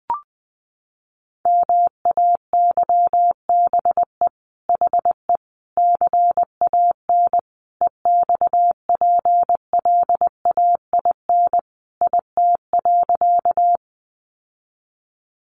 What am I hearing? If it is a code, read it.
Morse code, heard as a keyed tone, about 20 wpm, 700 Hz. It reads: MAYBE HE CAN EXPLAIN IT.
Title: Sentences from Top 1100 Words 20wpm